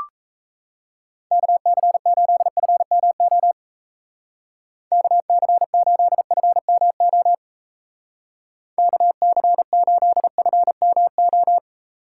Morse code, heard as a keyed tone, about 36 wpm, 700 Hz. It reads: KC8FMO KC8FMO KC8FMO